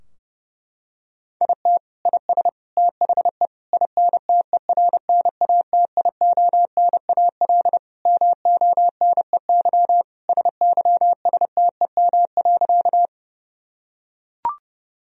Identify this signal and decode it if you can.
Morse code, heard as a keyed tone, about 30 wpm, 700 Hz. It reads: IT IS THE INTERNATIONAL MONEY SYSTEM.
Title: Sentences from Top 1000 Words - Review Part 4 - 30wpm